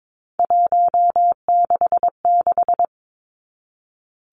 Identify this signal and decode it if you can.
Morse code, heard as a keyed tone, about 22 wpm, 700 Hz. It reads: 166
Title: Three Challenging Letter-Number Combinations 22wpm